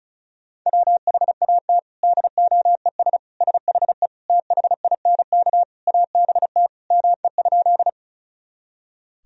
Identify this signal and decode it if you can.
Morse code, heard as a keyed tone, about 35 wpm, 700 Hz. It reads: WHAT DOES SHE THINK ABT ME?
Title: Sentences from Top 200 Words and Abbreviations 35wpm